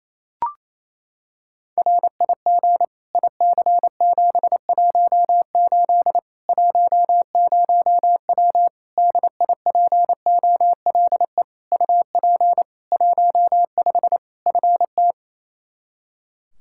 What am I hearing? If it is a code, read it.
Morse code, heard as a keyed tone, about 28 wpm, 700 Hz. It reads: RIG IC718 10W DIPOLE UP 15 FT
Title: Practice QSO - Exchange 2 - Info - 28wpm